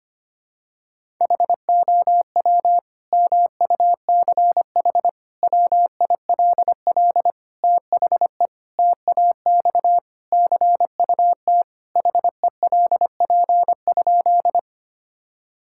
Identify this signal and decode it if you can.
Morse code, heard as a keyed tone, about 25 wpm, 700 Hz. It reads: HOW MUCH WILL THE TAX CUT HELP?
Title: Sentences from Top 1000 Words - Review Part 5 - 25wpm